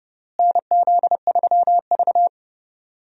Morse code, heard as a keyed tone, about 30 wpm, 700 Hz. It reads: NZ3V
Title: USA Call Signs - 4 Characters - Repeated 3x - 30wpm